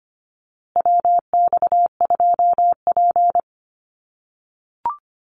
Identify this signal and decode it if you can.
Morse code, heard as a keyed tone, about 24 wpm, 700 Hz. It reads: WX2P
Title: USA Call Signs Repeated 3x 25wpm